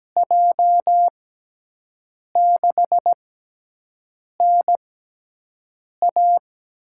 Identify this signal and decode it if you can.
Morse code, heard as a keyed tone, about 17 wpm, spaced as for 6 wpm, 700 Hz. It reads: J6NA
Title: USA Call Signs - 6 Characters - 6x Character Spacing - 17wpm